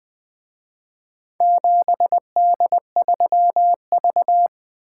Morse code, heard as a keyed tone, about 24 wpm, 700 Hz. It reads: 7D3V
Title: Speed-Racing - Four Letter-Number Combinations - 20wpm